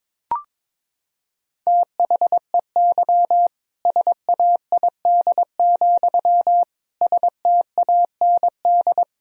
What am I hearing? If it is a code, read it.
Morse code, heard as a keyed tone, about 22 wpm, 700 Hz. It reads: THEY SAID, STAND
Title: Sentences from Top 1000 Words - Review Part 1 - 22wpm